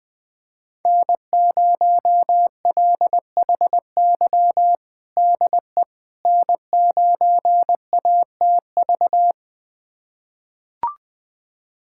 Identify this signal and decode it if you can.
Morse code, heard as a keyed tone, about 21 wpm, 700 Hz. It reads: N0LHY DE N9ATV
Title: USA Callsign DE Callsign - 5 Characters - 20wpm